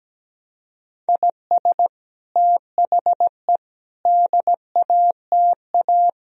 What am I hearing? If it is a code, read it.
Morse code, heard as a keyed tone, about 17 wpm, 700 Hz. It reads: IS THE DATA
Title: Sentences from Top 1000 Words - Review Part 3 - 17wpm